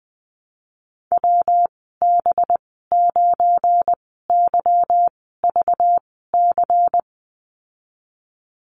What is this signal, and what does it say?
Morse code, heard as a keyed tone, about 19 wpm, spaced as for 14 wpm, 700 Hz. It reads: WB9YVC
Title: 200 Lesson 32g - Letter k - Call Signs 20wpm